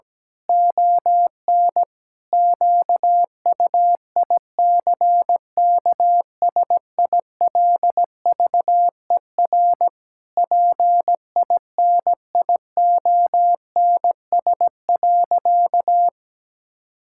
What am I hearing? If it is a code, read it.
Morse code, heard as a keyed tone, about 17 wpm, 700 Hz. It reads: ON QUICKSILVER PINIONS.
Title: Pangram Challenge v2 - 17wpm